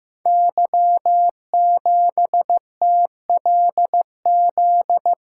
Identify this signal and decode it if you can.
Morse code, heard as a keyed tone, about 15 wpm, 700 Hz. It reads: Y7TLZ
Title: Code Groups - 5 Letter:Numbers - 15wpm